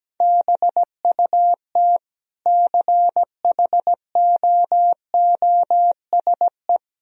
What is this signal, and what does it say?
Morse code, heard as a keyed tone, about 17 wpm, 700 Hz. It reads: BUT CHOOSE
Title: Sentences from Top 1000 Words - Review Part 4 - 17wpm